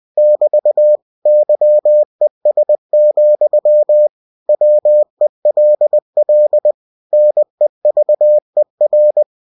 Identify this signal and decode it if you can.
Morse code, heard as a keyed tone, about 20 wpm, 600 Hz. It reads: <BT> YES, WELL NEVER
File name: cw-books / winnie-the-pooh / winnie-20wpm-page-0037